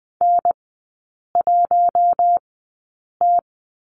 Morse code, heard as a keyed tone, about 20 wpm, spaced as for 8 wpm, 700 Hz. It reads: N1T
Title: 074 Lesson 13e - Letter L - Three Characters 20wpm